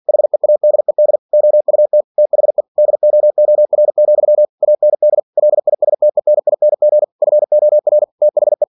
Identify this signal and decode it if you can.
Morse code, heard as a keyed tone, about 46 wpm, 600 Hz. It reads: HEADED OUT THE DOOR, AND LISTENING FOR THE
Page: Never 47wpm